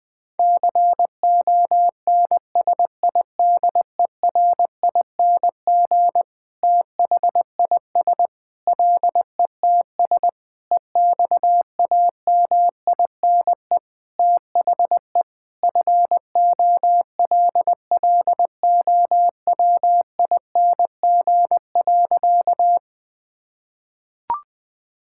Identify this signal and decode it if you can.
Morse code, heard as a keyed tone, about 20 wpm, 700 Hz. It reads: CONSIDERING THIS LETS EXAMINE THE FOLLOWING.
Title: Sentences with 9-Word Phrases - 20wpm